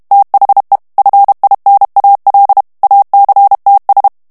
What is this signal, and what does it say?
Morse code, heard as a keyed tone, about 32 wpm, 800 Hz. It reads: THE FINAL ACTS
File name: ANGLAIS texte clair 3 vitesse 30 mots minute